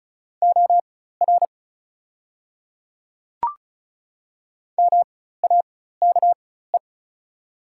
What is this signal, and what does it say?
Morse code, heard as a keyed tone, about 33 wpm, spaced as for 16 wpm, 690 Hz. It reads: OR MAKE
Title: 4x Character Spacing - Top 100 Words - 35wpm